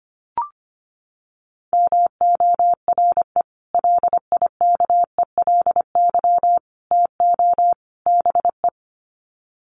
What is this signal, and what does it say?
Morse code, heard as a keyed tone, about 25 wpm, 700 Hz. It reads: MORE LIKELY TO BE
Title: Sets of 4 Words 25wpm